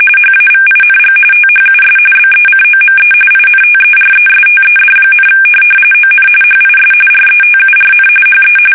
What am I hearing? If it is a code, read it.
RTTY, about 45.45 baud. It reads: HAVING THUS BECOME INTERESTED, ON HIS RETURN HE
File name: fsk850 / benjamin-g-lamme